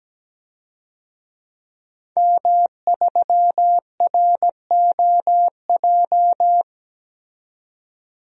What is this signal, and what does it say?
Morse code, heard as a keyed tone, about 17 wpm, 700 Hz. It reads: M3ROJ
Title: DXCC-100 Call Signs - 5 Characters - 17wpm